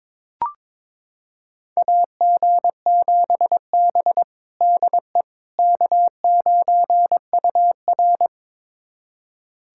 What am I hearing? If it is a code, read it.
Morse code, heard as a keyed tone, about 21 wpm, 700 Hz. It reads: AG7B DE K9UR
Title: USA Callsign DE Callsign - 4 Characters - 22wpm